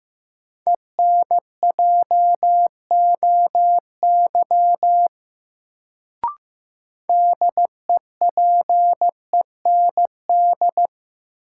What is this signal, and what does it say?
Morse code, heard as a keyed tone, about 15 wpm, 700 Hz. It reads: ENJOY DEPEND
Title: Sentences from Top 1300 Words - New Word Drill Alt - 15wpm